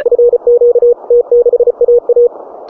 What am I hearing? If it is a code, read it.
Morse code, heard as a keyed tone, about 34 wpm, 480 Hz. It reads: FQ T6AA